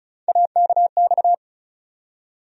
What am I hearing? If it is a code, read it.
Morse code, heard as a keyed tone, about 34 wpm, 700 Hz. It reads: AKX
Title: USA Call Sign Postfixes - 3 Letters 35wpm